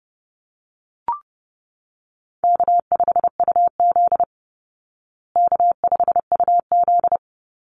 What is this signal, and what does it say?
Morse code, heard as a keyed tone, about 30 wpm, 700 Hz. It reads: K5UZ K5UZ
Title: Call Signs Repeated 2x - 30wpm